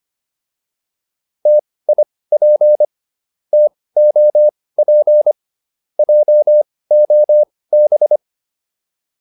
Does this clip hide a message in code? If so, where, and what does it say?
Yes: Morse code from 1.5 to 8.2 s, heard as a keyed tone, about 25 wpm, spaced as for 18 wpm, 600 Hz. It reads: TIP TOP JOB